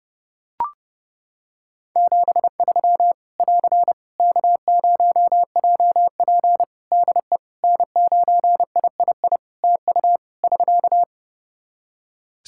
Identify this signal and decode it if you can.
Morse code, heard as a keyed tone, about 31 wpm, 700 Hz. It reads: 73 <AR> K0JP DE N9III TU <SK>
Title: Practice QSO - Exchange 4 - Ending - 30wpm